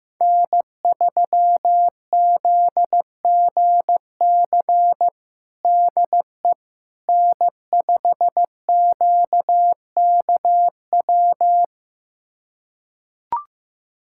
Morse code, heard as a keyed tone, about 15 wpm, 700 Hz. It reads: N3ZGC DE N5QKW
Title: USA Callsign DE Callsign - 5 Characters - 15wpm